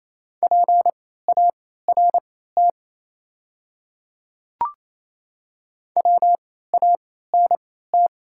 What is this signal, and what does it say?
Morse code, heard as a keyed tone, about 27 wpm, spaced as for 16 wpm, 700 Hz. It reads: PART WANT
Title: 3x Character Spacing - Top 300 Words - 28wpm